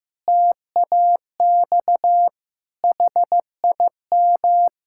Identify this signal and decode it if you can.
Morse code, heard as a keyed tone, about 15 wpm, 700 Hz. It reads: TAX HIM
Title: Sentences from Top 1000 Words - Review Part 5 - 15wpm